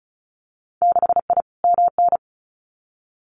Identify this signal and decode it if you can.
Morse code, heard as a keyed tone, about 36 wpm, 700 Hz. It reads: 6I MN
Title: Winter Field Day Exchanges Encore 35wpm